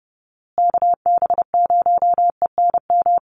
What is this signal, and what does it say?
Morse code, heard as a keyed tone, about 30 wpm, 700 Hz. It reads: KB0ENM
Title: USA Call Signs Repeated 2x 30wpm